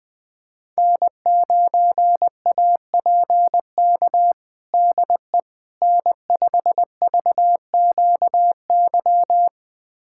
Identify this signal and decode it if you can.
Morse code, heard as a keyed tone, about 20 wpm, 700 Hz. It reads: N9APK DE N5VQY
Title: USA Callsign DE Callsign - 5 Characters - 20wpm